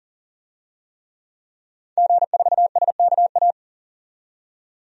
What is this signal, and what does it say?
Morse code, heard as a keyed tone, about 40 wpm, 700 Hz. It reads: G4SKA
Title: DXCC-100 Call Signs - 5 Characters - 40wpm